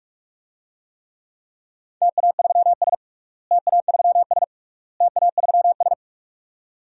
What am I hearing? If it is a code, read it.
Morse code, heard as a keyed tone, about 42 wpm, 700 Hz. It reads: TA3S TA3S TA3S
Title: DXCC-100 Call Signs - 4 Characters - Repeated 3x - 45wpm